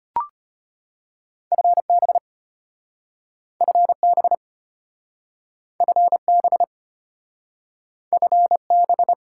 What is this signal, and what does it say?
Morse code, heard as a keyed tone, about 32 wpm, 700 Hz. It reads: FB FB FB FB